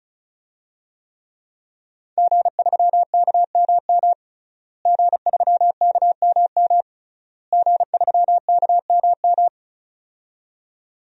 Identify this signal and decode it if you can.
Morse code, heard as a keyed tone, about 36 wpm, 700 Hz. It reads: G3KMM G3KMM G3KMM
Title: DXCC-100 Call Signs - 5 Characters - Repeated 3x - 35wpm